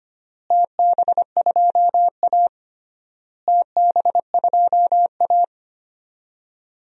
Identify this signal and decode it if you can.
Morse code, heard as a keyed tone, about 25 wpm, 700 Hz. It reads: TB2A TB2A
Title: DXCC-100 Call Signs - 4 Characters - Repeated 3x - 25wpm